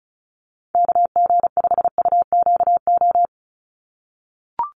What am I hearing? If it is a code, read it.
Morse code, heard as a keyed tone, about 34 wpm, 700 Hz. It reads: KG5UQO
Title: USA Call Signs 35wpm